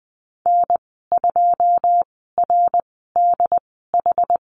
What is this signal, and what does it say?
Morse code, heard as a keyed tone, about 20 wpm, spaced as for 14 wpm, 700 Hz. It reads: N2RDH
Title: 082 Lesson 14g - Number 2 - Call Signs 20wpm